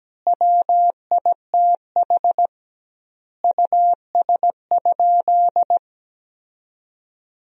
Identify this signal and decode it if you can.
Morse code, heard as a keyed tone, about 17 wpm, 700 Hz. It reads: WITH US?
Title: 2x Word-Spacing - Sentences from Top 200 Words - 17wpm